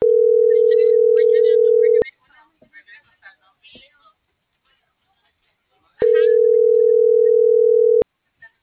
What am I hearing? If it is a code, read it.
Ringback tone (North America), 440 + 480 Hz.